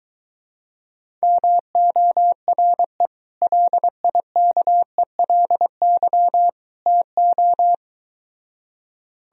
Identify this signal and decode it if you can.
Morse code, heard as a keyed tone, about 23 wpm, 700 Hz. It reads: MORE LIKELY TO